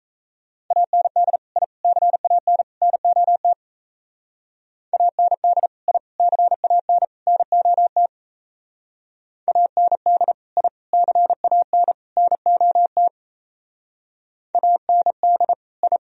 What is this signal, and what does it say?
Morse code, heard as a keyed tone, about 34 wpm, 700 Hz. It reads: AND I CAN NOT AND I CAN NOT AND I CAN NOT AND I
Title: Speed-Racing - Sets of 4 Words 28wpm